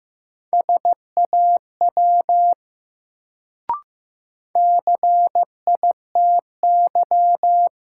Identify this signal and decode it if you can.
Morse code, heard as a keyed tone, about 15 wpm, 700 Hz. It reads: SAW CITY
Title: Sentences from Top 300 Words - New Word Drill - 15wpm